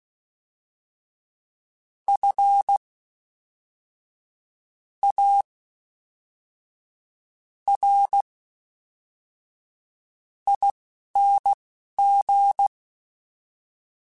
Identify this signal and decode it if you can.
Morse code, heard as a keyed tone, about 16 wpm, spaced as for 5 wpm, 800 Hz. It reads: FARING
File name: Special INGs / Special INGs-mp3s / Fast ing @10